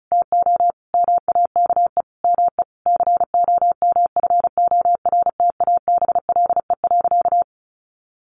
Morse code, heard as a keyed tone, about 34 wpm, 700 Hz. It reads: TO MAKE ME COMFORTABLE.
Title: Sentences from Top 1500 Words 35wpm